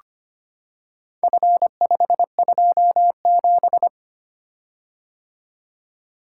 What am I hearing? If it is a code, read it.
Morse code, heard as a keyed tone, about 25 wpm, 700 Hz. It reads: F527
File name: practice / 25wpm / Four Letter-Number Combinations - 25wpm